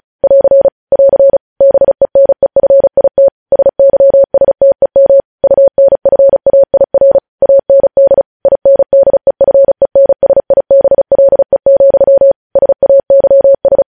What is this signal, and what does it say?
Morse code, heard as a keyed tone, about 36 wpm, 550 Hz. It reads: <AR> <AR> BENEFIT SYSTEM UNFAIR AND INDEFENSIBLE, SAYS
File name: podcast / News headlines-1x-35x35